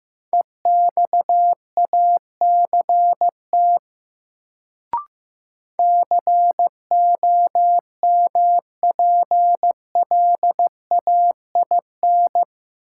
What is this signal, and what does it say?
Morse code, heard as a keyed tone, about 15 wpm, 700 Hz. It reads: EXACT COMPLAIN
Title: Sentences from Top 1300 Words - New Word Drill - 15wpm